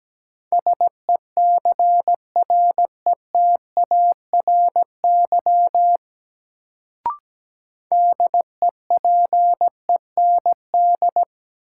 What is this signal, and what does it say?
Morse code, heard as a keyed tone, about 17 wpm, 700 Hz. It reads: SECRETARY DEPEND
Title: Sentences from Top 1300 Words - New Word Drill - 17wpm